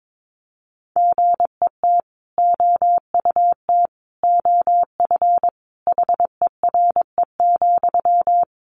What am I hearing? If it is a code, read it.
Morse code, heard as a keyed tone, about 22 wpm, 700 Hz. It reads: GET OUT OF HERE,
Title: A Sound Of Thunder 22wpm